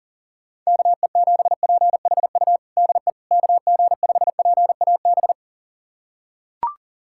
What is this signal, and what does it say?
Morse code, heard as a keyed tone, about 40 wpm, 700 Hz. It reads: KE7PHU DE KG5PAB
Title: USA Callsign DE Callsign - 6 Characters - 40wpm